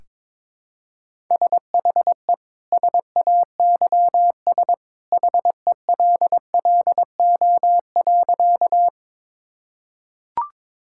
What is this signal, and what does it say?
Morse code, heard as a keyed tone, about 21 wpm, 700 Hz. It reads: SHE SAYS HELLO.
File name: practice / 22wpm / Sentences with 2-Word Phrases - 22wpm